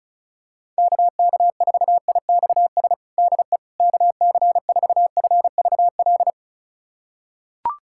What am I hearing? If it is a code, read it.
Morse code, heard as a keyed tone, about 36 wpm, 700 Hz. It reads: KK4IXS DE KC4FVL
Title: USA Callsign DE Callsign - 6 Characters - 35wpm